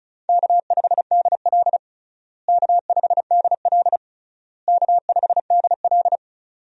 Morse code, heard as a keyed tone, about 36 wpm, 700 Hz. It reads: K5DL K5DL K5DL
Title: USA Call Signs - 4 Characters - Repeated 3x - 35wpm